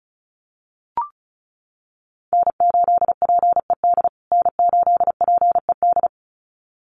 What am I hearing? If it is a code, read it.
Morse code, heard as a keyed tone, about 34 wpm, 700 Hz. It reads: N8PED N8PED
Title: USA Call Signs Repeated 2x 35wpm